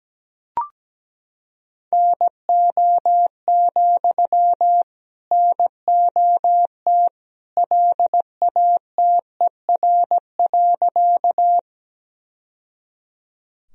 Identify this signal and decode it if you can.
Morse code, heard as a keyed tone, about 17 wpm, 700 Hz. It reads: NO, NOT LATER.
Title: Sentences from Top 1000 Words - Review Part 4 - 17wpm